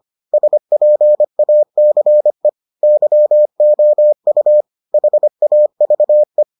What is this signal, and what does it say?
Morse code, heard as a keyed tone, about 25 wpm, 600 Hz. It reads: SPACE YOU HAVE